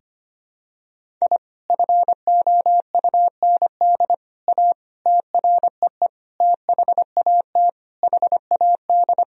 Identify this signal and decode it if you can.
Morse code, heard as a keyed tone, about 25 wpm, 700 Hz. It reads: I FOUND A TREE THAT HAD
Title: Sentences from Top 1000 Words - Review Part 1 - 25wpm